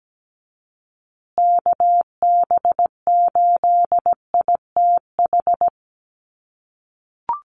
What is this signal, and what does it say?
Morse code, heard as a keyed tone, about 17 wpm, 700 Hz. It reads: KB8ITH
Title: USA Call Signs Repeated 2x 17wpm